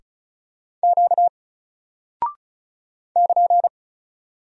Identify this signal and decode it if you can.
Morse code, heard as a keyed tone, about 35 wpm, 700 Hz. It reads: Q <KN>
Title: Single Character (Letter:Number:Punctuation:Prosign) 35wpm